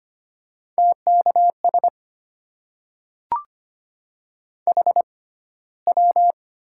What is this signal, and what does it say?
Morse code, heard as a keyed tone, about 25 wpm, spaced as for 15 wpm, 700 Hz. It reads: TKS HW
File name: Top 90 QSO Elements - 6x Character Spacing - 25wpm